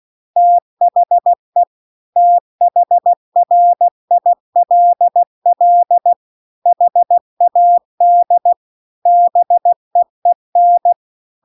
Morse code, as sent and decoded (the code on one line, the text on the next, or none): - .... / - .... .-. .. .-.. .-.. / .... .- -.. / -... . -.
THE THRILL HAD BEEN